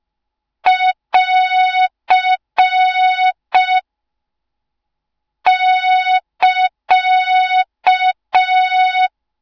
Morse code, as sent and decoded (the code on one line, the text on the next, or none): .-.-. / -.-.-
<AR> <KA>